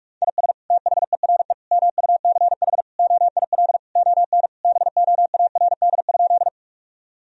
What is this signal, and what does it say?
Morse code, heard as a keyed tone, about 46 wpm, 700 Hz. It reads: IS THERE MUCH OIL ON BOARD?
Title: Sentences from Top 1000 Words - Review Part 4 - 45wpm